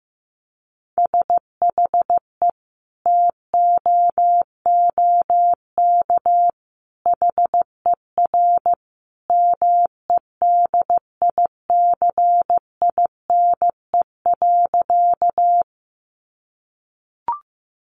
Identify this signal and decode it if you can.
Morse code, heard as a keyed tone, about 15 wpm, 700 Hz. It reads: SHE TOOK HER MEDICINE.
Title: Sentences from Top 900 Words 15wpm